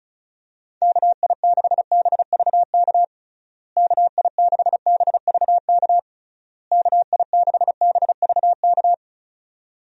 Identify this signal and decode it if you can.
Morse code, heard as a keyed tone, about 34 wpm, 700 Hz. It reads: KI6BVK KI6BVK KI6BVK
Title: USA Call Signs - 6 Characters - Repeated 3x - 35wpm